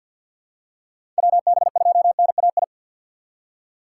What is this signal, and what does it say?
Morse code, heard as a keyed tone, about 47 wpm, 700 Hz. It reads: WB2NAI
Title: USA Call Signs - 6 Characters - 3x Character Spacing - 50wpm